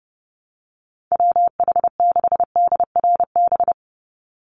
Morse code, heard as a keyed tone, about 28 wpm, 700 Hz. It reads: WH6DRB
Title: USA Call Signs 30wpm